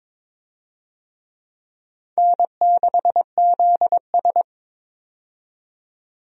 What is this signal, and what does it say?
Morse code, heard as a keyed tone, about 22 wpm, 700 Hz. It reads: N6ZS
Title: USA Call Signs - 4 Characters - 22wpm